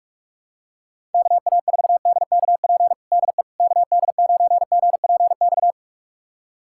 Morse code, heard as a keyed tone, about 47 wpm, 700 Hz. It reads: KA4DKP DE KD9GPX
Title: USA Callsign DE Callsign 45wpm